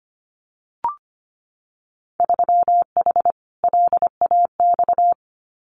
Morse code, heard as a keyed tone, about 25 wpm, 700 Hz. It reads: 3H LAX
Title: Winter Field Day Exchanges 25wpm